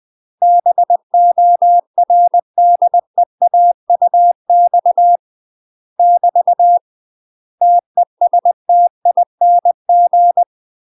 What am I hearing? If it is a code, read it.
Morse code, heard as a keyed tone, about 20 wpm, 700 Hz. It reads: BORDEAUX <BT> TESTING
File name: UK MASTER QSOs TRAINING / 20WPM / UK MASTER QSOs 0043